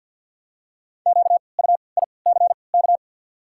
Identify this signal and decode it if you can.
Morse code, heard as a keyed tone, about 53 wpm, spaced as for 29 wpm, 700 Hz. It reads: QUICK
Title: 3x Character Spacing - Top 500 Words - 50wpm